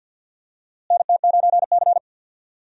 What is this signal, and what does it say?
Morse code, heard as a keyed tone, about 51 wpm, 700 Hz. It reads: NT9C
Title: USA Call Signs - 4 Characters - Repeated 2x - 50wpm